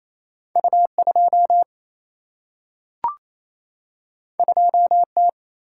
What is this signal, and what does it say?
Morse code, heard as a keyed tone, about 28 wpm, 700 Hz. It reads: U2 2T
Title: Single-Letter Single-Number 28wpm